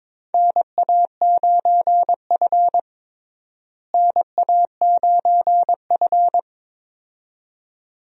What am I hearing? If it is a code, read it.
Morse code, heard as a keyed tone, about 22 wpm, 700 Hz. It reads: NA9F NA9F
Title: USA Call Signs - 4 Characters - Repeated 2x - 22wpm